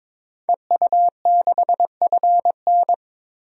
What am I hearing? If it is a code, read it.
Morse code, heard as a keyed tone, about 21 wpm, 700 Hz. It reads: EU6FN
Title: DXCC-100 Call Signs - 5 Characters - 22wpm